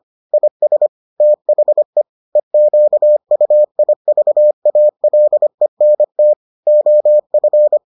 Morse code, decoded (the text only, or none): IS THE EQUIVALENT OF